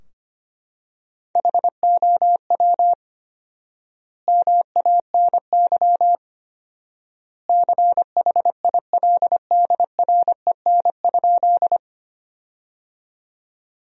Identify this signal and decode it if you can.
Morse code, heard as a keyed tone, about 25 wpm, 700 Hz. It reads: HOW MANY CHILDREN?
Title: 4x Word-Spacing - Sentences from Top 300 Words - 25wpm